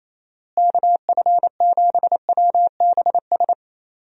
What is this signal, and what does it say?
Morse code, heard as a keyed tone, about 28 wpm, 700 Hz. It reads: KF7WBS
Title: USA Call Signs - 6 Characters - Repeated 3x - 28wpm